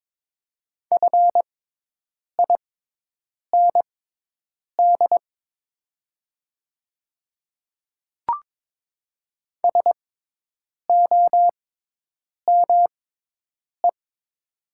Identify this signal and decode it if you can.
Morse code, heard as a keyed tone, about 22 wpm, spaced as for 8 wpm, 700 Hz. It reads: FIND SOME
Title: 6x Character Spacing - Top 100 Words - 22wpm